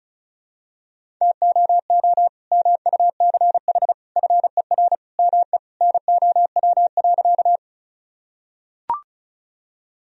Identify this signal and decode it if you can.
Morse code, heard as a keyed tone, about 36 wpm, 700 Hz. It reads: TOO MUCH FER ME NOW.
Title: Sentences from Top 200 Words and Abbreviations 35wpm